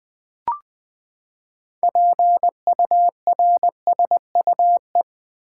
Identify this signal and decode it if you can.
Morse code, heard as a keyed tone, about 20 wpm, 700 Hz. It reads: PURSUE
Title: Six Letter Words - 20wpm